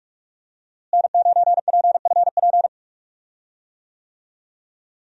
Morse code, heard as a keyed tone, about 45 wpm, 700 Hz. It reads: N9PFP
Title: USA Call Signs - 5 Characters - Repeated 2x - 45wpm